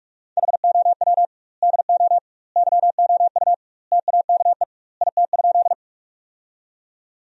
Morse code, heard as a keyed tone, about 46 wpm, 700 Hz. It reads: HOW DO YOU TAKE IT?
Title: 2x Word-Spacing - Sentences from Top 200 Words - 45wpm